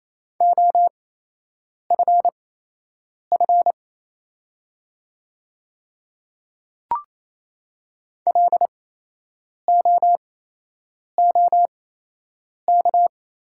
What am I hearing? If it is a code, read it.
Morse code, heard as a keyed tone, about 27 wpm, spaced as for 8 wpm, 690 Hz. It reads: OFF LOOK
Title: 8x Character Spacing - Top 200 Words - 28wpm